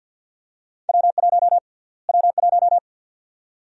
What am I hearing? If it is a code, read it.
Morse code, heard as a keyed tone, about 53 wpm, 700 Hz. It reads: W1 W1
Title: USA Call Sign Prefixes - Repeated 2x - 50wpm